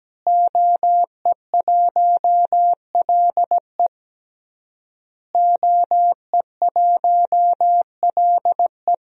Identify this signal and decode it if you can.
Morse code, heard as a keyed tone, about 17 wpm, 700 Hz. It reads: OE1LE OE1LE
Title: DXCC-100 Call Signs - 5 Characters - Repeated 3x - 17wpm